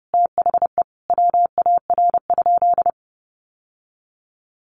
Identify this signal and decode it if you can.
Morse code, heard as a keyed tone, about 30 wpm, 700 Hz. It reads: THE WAR?
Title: Sentences from Top 1200 Words 30wpm